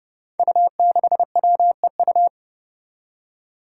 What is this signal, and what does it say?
Morse code, heard as a keyed tone, about 32 wpm, 700 Hz. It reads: U6WEU
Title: Five Challenging Letter-Number Combinations 30wpm